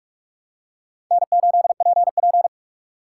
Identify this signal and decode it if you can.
Morse code, heard as a keyed tone, about 46 wpm, 700 Hz. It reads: N8PP
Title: USA Call Signs - 4 Characters - 3x Character Spacing - 45wpm